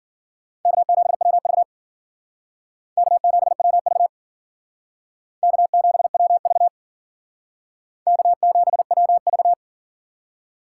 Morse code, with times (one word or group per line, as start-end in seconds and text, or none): 0.7-1.6 K7WV
3.0-4.1 K7WV
5.4-6.7 K7WV
8.1-9.5 K7WV